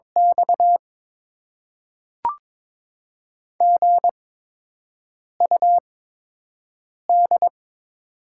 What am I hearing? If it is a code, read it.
Morse code, heard as a keyed tone, about 22 wpm, spaced as for 6 wpm, 700 Hz. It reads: X GUD
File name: Top 20 QSO Elements - 8x Character Spacing - 22wpm